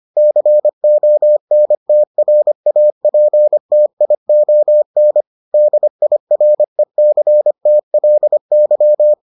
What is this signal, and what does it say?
Morse code, heard as a keyed tone, about 25 wpm, 600 Hz. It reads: CONTRAPTION DIRECTLY